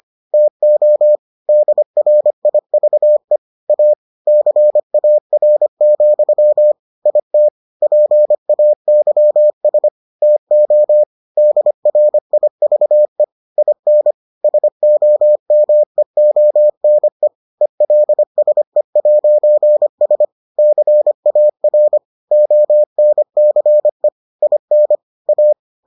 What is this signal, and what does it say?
Morse code, heard as a keyed tone, about 25 wpm, 600 Hz. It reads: TO DRIVE A CAR, IT PAYS TO DRIVE IN SOMEONE ELSE'S CAR ONCE IN A